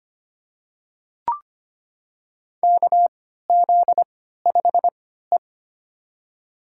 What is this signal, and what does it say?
Morse code, heard as a keyed tone, about 25 wpm, spaced as for 14 wpm, 700 Hz. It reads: KZ5E
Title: USA Call Signs - 4 Characters - 3x Character Spacing - 25wpm